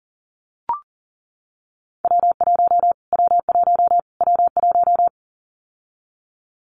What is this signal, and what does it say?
Morse code, heard as a keyed tone, about 40 wpm, 700 Hz. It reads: W1 W1 W1
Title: Callsign Prefixes for Top 25 DXCC - Repeated 3x - 40wpm